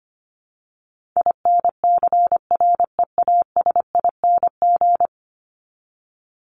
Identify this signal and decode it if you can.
Morse code, heard as a keyed tone, about 25 wpm, 700 Hz. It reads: INCREASING
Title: ING Suffix - Top 100 Words with Base Words 25wpm